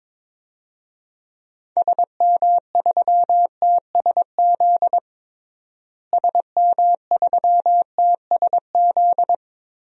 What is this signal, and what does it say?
Morse code, heard as a keyed tone, about 22 wpm, 700 Hz. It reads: SM3TSZ SM3TSZ
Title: DXCC-100 Call Signs - 6 Characters - Repeated 2x - 22wpm